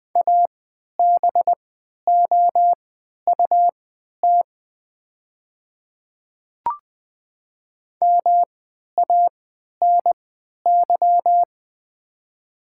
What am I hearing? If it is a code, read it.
Morse code, heard as a keyed tone, about 20 wpm, spaced as for 11 wpm, 690 Hz. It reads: ABOUT MANY
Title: 3x Character Spacing - Top 100 Words - 20wpm